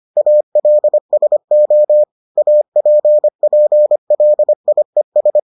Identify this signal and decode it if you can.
Morse code, heard as a keyed tone, about 25 wpm, 600 Hz. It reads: ALSO APPLIES